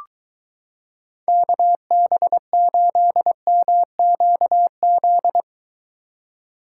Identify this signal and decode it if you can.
Morse code, heard as a keyed tone, about 22 wpm, 700 Hz. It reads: KB8MQZ